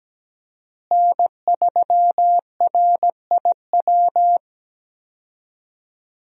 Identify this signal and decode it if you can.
Morse code, heard as a keyed tone, about 17 wpm, 700 Hz. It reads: N3RIW
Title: USA Call Signs - 5 Characters - Repeated 3x - 17wpm